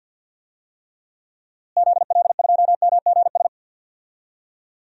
Morse code, heard as a keyed tone, about 53 wpm, 700 Hz. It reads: ZL2MGS
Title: DXCC-100 Call Signs - 6 Characters - Repeated 2x - 50wpm